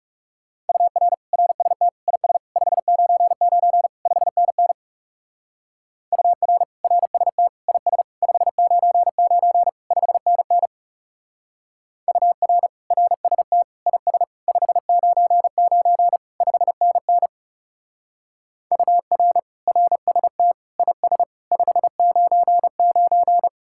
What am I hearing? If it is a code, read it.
Morse code, heard as a keyed tone, about 35 wpm, 700 Hz. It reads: UR RST IS 599 5NN UR RST IS 599 5NN UR RST IS 599 5NN UR RST IS 599
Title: Speed-Racing Common QSO Phrases - 30wpm